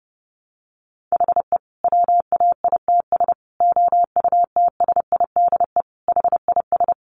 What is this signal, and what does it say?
Morse code, heard as a keyed tone, about 30 wpm, 700 Hz. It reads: HE WAITS OUTSIDE HIS
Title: Additional Easy Phrases 30wpm